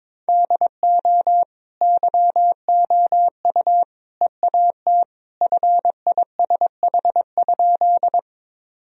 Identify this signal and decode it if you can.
Morse code, heard as a keyed tone, about 22 wpm, 700 Hz. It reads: DO YOU EAT FISH?
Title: Sentences from Top 1000 Words - Review Part 1 - 22wpm